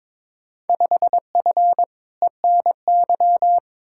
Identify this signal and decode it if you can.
Morse code, heard as a keyed tone, about 22 wpm, 700 Hz. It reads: 5F ENY
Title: Field Day Exchanges Encore - 22wpm